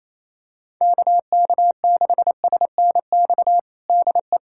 Morse code, heard as a keyed tone, about 28 wpm, 700 Hz. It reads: KK6SNX DE
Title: USA Callsign DE Callsign - 6 Characters - 28wpm